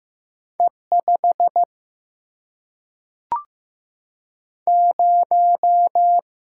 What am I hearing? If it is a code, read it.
Morse code, heard as a keyed tone, about 15 wpm, 700 Hz. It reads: E 5 0
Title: Single-Letter Single-Number 15wpm